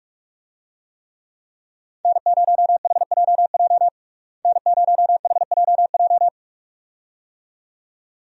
Morse code, heard as a keyed tone, about 42 wpm, 700 Hz. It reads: N0HJJ N0HJJ
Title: USA Call Signs - 5 Characters - Repeated 2x - 45wpm